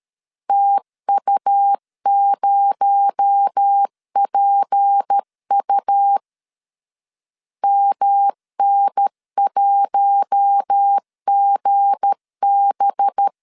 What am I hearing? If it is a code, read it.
Morse code, heard as a keyed tone, about 13 wpm, 800 Hz. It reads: TU0PU MN1GB
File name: LEZIONI / 7 GF9Z / 7 G 12wpm 2055